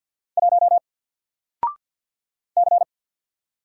Morse code, heard as a keyed tone, about 50 wpm, 700 Hz. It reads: 1 C
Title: Single Letter-Number 50wpm